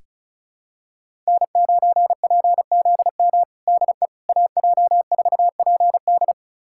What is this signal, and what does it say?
Morse code, heard as a keyed tone, about 36 wpm, 700 Hz. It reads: N9PZM DE AJ4PD
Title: USA Callsign DE Callsign - 5 Characters - 35wpm